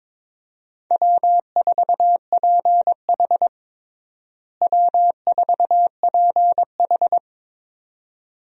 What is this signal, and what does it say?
Morse code, heard as a keyed tone, about 22 wpm, 700 Hz. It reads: W4PH W4PH
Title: USA Call Signs - 4 Characters - Repeated 2x - 22wpm